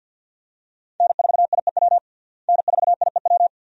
Morse code, heard as a keyed tone, about 53 wpm, 700 Hz. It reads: N4IEW N4IEW
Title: USA Call Signs - 5 Characters - Repeated 2x - 50wpm